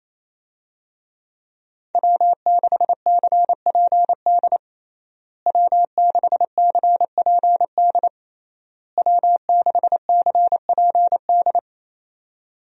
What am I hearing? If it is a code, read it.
Morse code, heard as a keyed tone, about 28 wpm, 700 Hz. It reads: W6CPD W6CPD W6CPD
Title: USA Call Signs - 5 Characters - Repeated 3x - 28wpm